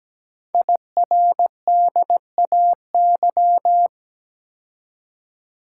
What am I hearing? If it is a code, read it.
Morse code, heard as a keyed tone, about 17 wpm, 700 Hz. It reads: IRDAY
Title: Code Groups - 5 Letters - 17wpm